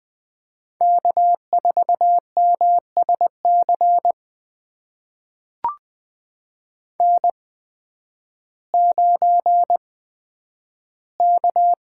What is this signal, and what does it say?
Morse code, heard as a keyed tone, about 20 wpm, spaced as for 11 wpm, 700 Hz. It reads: K4MSC N9K
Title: USA Call Signs - 5 Characters - 8x Character Spacing - 20wpm